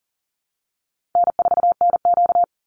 Morse code, heard as a keyed tone, about 40 wpm, 700 Hz. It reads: N4NQ
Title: Easy Call Signs - 40wpm